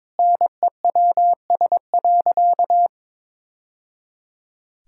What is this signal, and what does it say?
Morse code, heard as a keyed tone, about 22 wpm, 700 Hz. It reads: NEWS.
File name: practice / 22wpm / Sentences with 3-Word Phrases - 22wpm